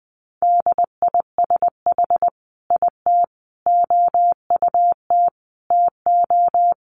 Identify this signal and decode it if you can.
Morse code, heard as a keyed tone, about 19 wpm, 700 Hz. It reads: DISH IT OUT TO
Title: Sentences from Top 1300 Words 20wpm